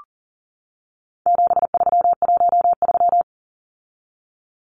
Morse code, heard as a keyed tone, about 40 wpm, 700 Hz. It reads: 7313
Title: Numbers - 4-Digit 40wpm